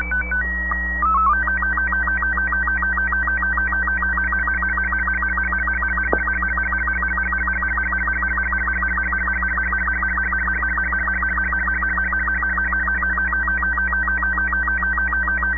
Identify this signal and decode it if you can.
SSTV (slow-scan television), Robot 36.
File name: sat / AO-51 14Jul2010 SSTV